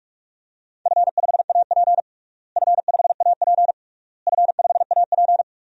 Morse code, heard as a keyed tone, about 46 wpm, 700 Hz. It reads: F5AP F5AP F5AP
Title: DXCC-100 Call Signs - 4 Characters - Repeated 3x - 45wpm